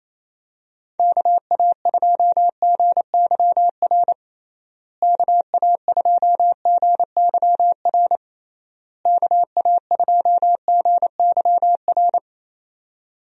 Morse code, heard as a keyed tone, about 28 wpm, 700 Hz. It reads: KA2GYR KA2GYR KA2GYR
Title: USA Call Signs - 6 Characters - Repeated 3x - 28wpm